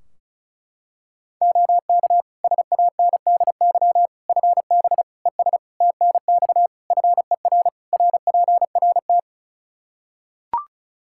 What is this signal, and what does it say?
Morse code, heard as a keyed tone, about 36 wpm, 700 Hz. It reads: OK SANDY FB ES TNX FER RPRT
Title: K1USN SST QSO Practice 35wpm